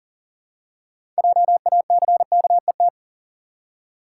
Morse code, heard as a keyed tone, about 40 wpm, 700 Hz. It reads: JACKET
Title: Six Letter Words - 40wpm